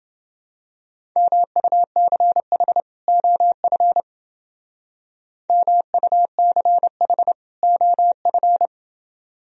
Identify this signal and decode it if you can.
Morse code, heard as a keyed tone, about 30 wpm, 700 Hz. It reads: MUCH OF MUCH OF